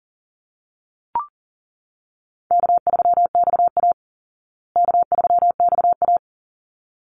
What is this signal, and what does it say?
Morse code, heard as a keyed tone, about 40 wpm, 700 Hz. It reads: K3XA K3XA
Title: Call Signs Repeated 3x - 40wpm